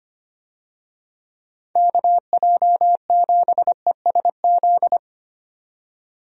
Morse code, heard as a keyed tone, about 25 wpm, 700 Hz. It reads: KJ7ESZ
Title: USA Call Signs - 6 Characters - 25wpm